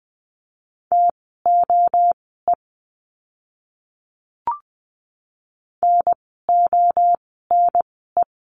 Morse code, heard as a keyed tone, about 20 wpm, spaced as for 14 wpm, 700 Hz. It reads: TOE NONE
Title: 030 Lesson 6f - Letter i - Words 20wpm